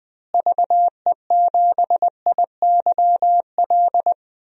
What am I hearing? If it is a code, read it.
Morse code, heard as a keyed tone, about 20 wpm, 700 Hz. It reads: VE7IYL
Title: DXCC-100 Call Signs - 6 Characters - Repeated 3x - 20wpm